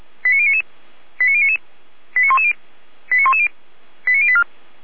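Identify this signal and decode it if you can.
Five-tone selective call (ZVEI), heard as quick runs of five tones, tones of about 70 ms. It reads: 89099 89110 89110 89084
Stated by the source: ZVEI five-tone selective call received by SDR.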